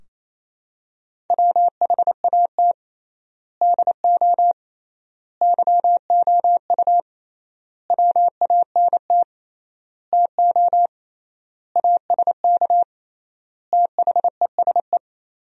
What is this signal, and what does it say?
Morse code, heard as a keyed tone, about 27 wpm, 700 Hz. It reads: WHAT DO YOU WANT TO ASK THESE
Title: 3x Word-Spacing - Sentences from Top 200 Words - 28wpm